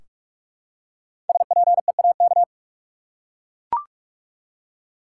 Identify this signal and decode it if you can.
Morse code, heard as a keyed tone, about 42 wpm, 700 Hz. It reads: SPEAK
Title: Five Letter Words - 45wpm